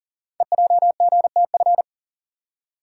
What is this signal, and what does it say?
Morse code, heard as a keyed tone, about 42 wpm, 700 Hz. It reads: EJGTF
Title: Code Groups - 5 Letters - 40wpm